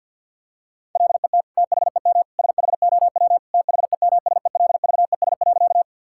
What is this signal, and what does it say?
Morse code, heard as a keyed tone, about 47 wpm, 700 Hz. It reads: LET THEM SHOW THEMSELVES.
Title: Sentences from Top 1000 Words - Review Part 5 - 50wpm